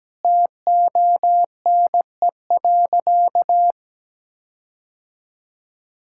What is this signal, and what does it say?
Morse code, heard as a keyed tone, about 17 wpm, 700 Hz. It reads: TONE.
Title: Sentences from Top 1600 Words 17wpm